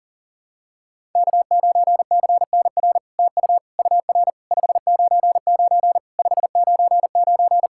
Morse code, heard as a keyed tone, about 42 wpm, 700 Hz. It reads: K9CNR TU UR 599 599
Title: SOTA QSO Practice 40wpm